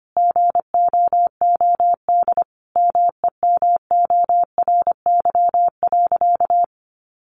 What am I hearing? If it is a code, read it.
Morse code, heard as a keyed tone, about 25 wpm, 700 Hz. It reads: GOOD MEMORY.
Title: Sentences from Top 1300 Words 25wpm